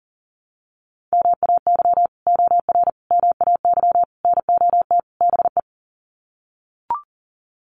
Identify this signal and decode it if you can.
Morse code, heard as a keyed tone, about 40 wpm, 700 Hz. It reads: MAY OR MAY NOT BE
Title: Sets of 5 Words 40wpm